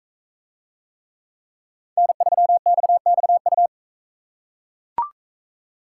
Morse code, heard as a keyed tone, about 40 wpm, 700 Hz. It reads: N3XXU